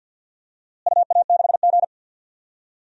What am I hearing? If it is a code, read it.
Morse code, heard as a keyed tone, about 48 wpm, 700 Hz. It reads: UA6G
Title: DXCC-100 Call Signs - 4 Characters - Repeated 2x - 50wpm